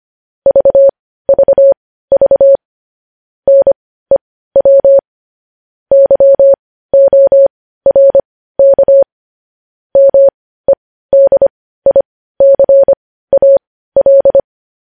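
Morse code, heard as a keyed tone, about 25 wpm, spaced as for 15 wpm, 550 Hz. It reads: VVV NEW YORK MEDICAL
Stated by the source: Morse code news headlines at 15 WPM.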